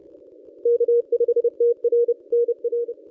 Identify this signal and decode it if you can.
Morse code, heard as a keyed tone, about 29 wpm, 440 Hz. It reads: K5TR NR
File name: contests / ss-cw-k5tr-op-n5rz -1000